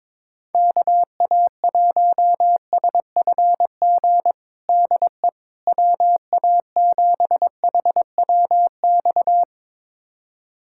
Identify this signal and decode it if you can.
Morse code, heard as a keyed tone, about 22 wpm, 700 Hz. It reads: KA1SFG DE WA7HWX
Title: USA Callsign DE Callsign 22wpm